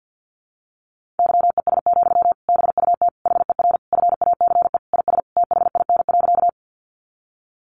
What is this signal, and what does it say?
Morse code, heard as a keyed tone, about 50 wpm, 700 Hz. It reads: YES, BUT HER FACE IS THIN.
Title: Sentences from Top 1200 Words 50wpm